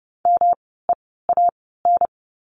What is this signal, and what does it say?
Morse code, heard as a keyed tone, about 30 wpm, spaced as for 17 wpm, 700 Hz. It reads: MEAN
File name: course / 30:10wpm / 108 Lesson 18f - Letter m - Words 30wpm